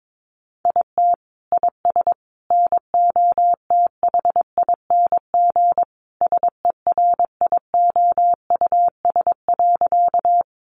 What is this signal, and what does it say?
Morse code, heard as a keyed tone, about 22 wpm, 700 Hz. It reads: IT IS NOTHING SERIOUS.
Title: Sentences from Top 1100 Words 22wpm